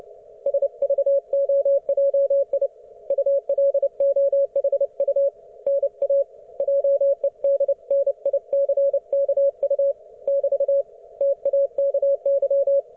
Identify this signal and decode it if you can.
Morse code, heard as a keyed tone, about 29 wpm, 560 Hz. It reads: SVOJI ULOHU NA JEDNICKU <BT> TAKY